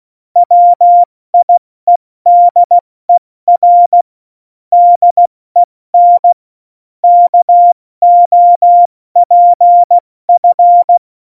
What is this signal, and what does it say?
Morse code, heard as a keyed tone, about 16 wpm, spaced as for 14 wpm, 700 Hz. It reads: WIEDER DEN KOPF
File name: Morsekurs / Witze / Witz 21 Juli 2020, 70 F80